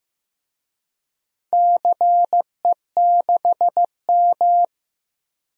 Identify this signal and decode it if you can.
Morse code, heard as a keyed tone, about 15 wpm, 700 Hz. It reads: CE6M
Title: DXCC-100 Call Signs - 4 Characters - 15wpm